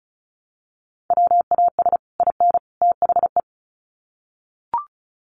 Morse code, heard as a keyed tone, about 34 wpm, 700 Hz. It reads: WAS IN THE
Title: Sets of 3 Words 35wpm